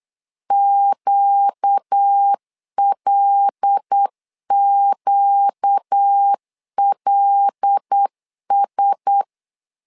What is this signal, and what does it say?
Morse code, heard as a keyed tone, about 8 wpm, 800 Hz. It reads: QLQLS
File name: LEZIONI / 5 LBVC / 5 L 8wpm 2831